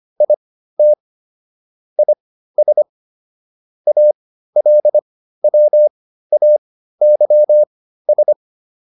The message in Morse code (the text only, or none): IT IS ALWAYS